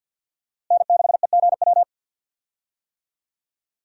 Morse code, heard as a keyed tone, about 52 wpm, 700 Hz. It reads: N6EGW
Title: USA Call Signs - 5 Characters - Repeated 3x - 50wpm